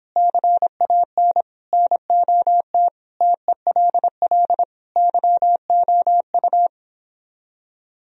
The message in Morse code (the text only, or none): CAN NOT TELL YOU